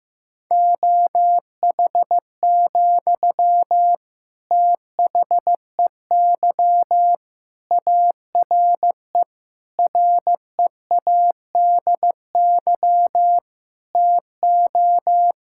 Morse code, heard as a keyed tone, about 16 wpm, 700 Hz. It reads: OH, THEY ARE READY TO
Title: Sentences from Top 1600 Words 15wpm